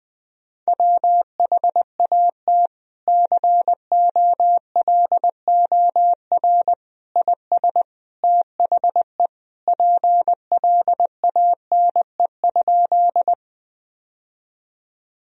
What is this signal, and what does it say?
Morse code, heard as a keyed tone, about 21 wpm, 700 Hz. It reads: WHAT COLOR IS THE PLANE?
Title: Sentences from Top 1000 Words - Review Part 3 - 20wpm